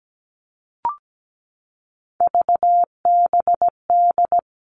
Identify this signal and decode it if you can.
Morse code, heard as a keyed tone, about 17 wpm, 700 Hz. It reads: VBD
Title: All Cut Numbers - 3-Digit 17wpm